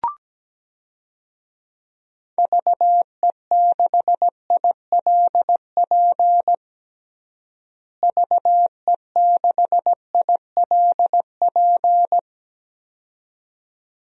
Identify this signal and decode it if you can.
Morse code, heard as a keyed tone, about 17 wpm, 700 Hz. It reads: VE6ILP VE6ILP
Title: DXCC-100 Call Signs - 6 Characters - Repeated 2x - 17wpm